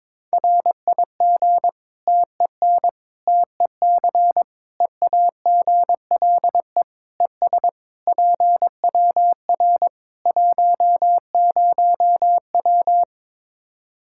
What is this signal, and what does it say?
Morse code, heard as a keyed tone, about 22 wpm, 700 Hz. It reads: RIG TEN TEC EAGLE ES PWR 10W
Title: K1USN SST QSO Practice 22wpm